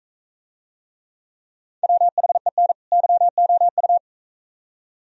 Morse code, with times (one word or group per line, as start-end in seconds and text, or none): 1.8-2.7 WHEN
2.9-4.0 YOU